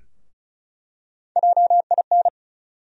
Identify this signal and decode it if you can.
Morse code, heard as a keyed tone, about 34 wpm, 700 Hz. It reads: JIN
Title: USA Call Sign Postfixes - 3 Letters 35wpm